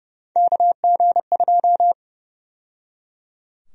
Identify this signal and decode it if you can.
Morse code, heard as a keyed tone, about 30 wpm, 700 Hz. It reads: KG2
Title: USA Call Sign Prefixes - 30wpm